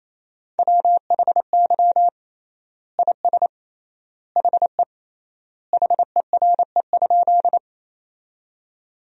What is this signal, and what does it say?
Morse code, heard as a keyed tone, about 27 wpm, 700 Hz. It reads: WHY IS HE HERE?
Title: 3x Word-Spacing - Sentences from Top 200 Words - 28wpm